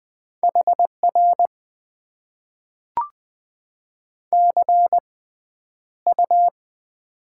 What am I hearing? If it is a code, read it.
Morse code, heard as a keyed tone, about 20 wpm, spaced as for 10 wpm, 700 Hz. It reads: HR CU
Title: Top 90 QSO Elements - 6x Character Spacing - 20wpm